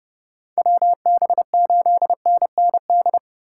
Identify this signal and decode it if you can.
Morse code, heard as a keyed tone, about 35 wpm, 700 Hz. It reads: WB8NND
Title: Speed Racing - USA Call Signs 30wpm